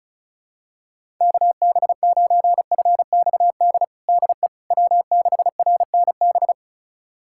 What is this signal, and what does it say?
Morse code, heard as a keyed tone, about 36 wpm, 700 Hz. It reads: KB9FXD DE W6RNB
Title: USA Callsign DE Callsign 35wpm